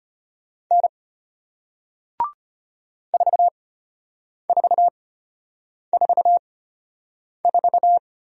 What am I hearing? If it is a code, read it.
Morse code, heard as a keyed tone, about 34 wpm, spaced as for 8 wpm, 700 Hz. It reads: N 4444